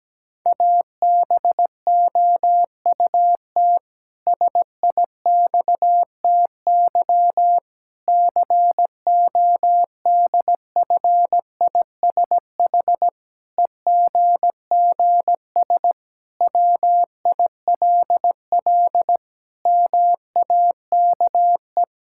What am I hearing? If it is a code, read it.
Morse code, heard as a keyed tone, about 17 wpm, 700 Hz. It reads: ABOUT SIXTY CODFISH EGGS WILL MAKE
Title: Pangram Challenge v2 - 17wpm